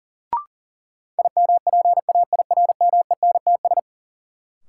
Morse code, heard as a keyed tone, about 40 wpm, 700 Hz. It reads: IMPAIRMENTS
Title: Sets of 5 Words Encore - New Word Drill - 40wpm